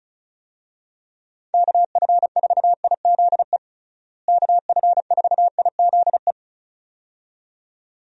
Morse code, heard as a keyed tone, about 36 wpm, 700 Hz. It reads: KF4IZE KF4IZE
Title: USA Call Signs - 6 Characters - Repeated 2x - 35wpm